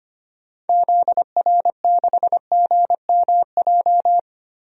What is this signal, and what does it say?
Morse code, heard as a keyed tone, about 25 wpm, 700 Hz. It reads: ZR6GMJ
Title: DXCC-100 Call Signs - 6 Characters - Repeated 3x - 25wpm